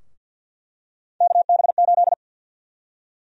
Morse code, heard as a keyed tone, about 50 wpm, 700 Hz. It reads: KB8
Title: USA Call Sign Prefixes - Repeated 2x - 50wpm